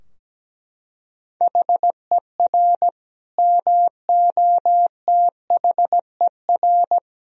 Morse code, heard as a keyed tone, about 17 wpm, 700 Hz. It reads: HER MOTHER
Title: Sentences from Top 1000 Words - Review Part 3 - 17wpm